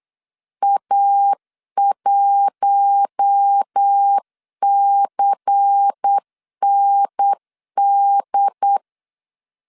Morse code, heard as a keyed tone, about 8 wpm, 800 Hz. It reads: A1CND
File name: LEZIONI / 6 PKX0 / 6 PKX 8wpm 3041